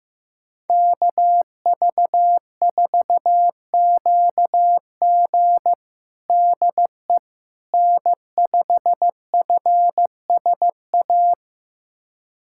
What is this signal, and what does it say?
Morse code, heard as a keyed tone, about 15 wpm, 700 Hz. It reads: KV4QG DE N5FSA
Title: USA Callsign DE Callsign - 5 Characters - 15wpm